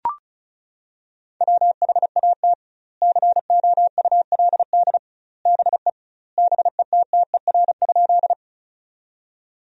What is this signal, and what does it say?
Morse code, heard as a keyed tone, about 36 wpm, 700 Hz. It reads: WHAT COULD BE BETTER?
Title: 2x Word-Spacing - Sentences from Top 400 Words - 35wpm